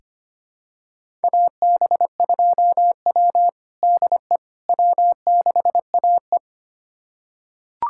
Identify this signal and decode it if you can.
Morse code, heard as a keyed tone, about 25 wpm, 700 Hz. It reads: AB2W DE W6AE
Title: USA Callsign DE Callsign - 4 Characters - 25wpm